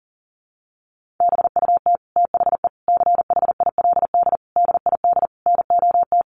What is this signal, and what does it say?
Morse code, heard as a keyed tone, about 40 wpm, 700 Hz. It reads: BUT THE CHILD DID NOT
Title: Sentences from Top 1400 Words 40wpm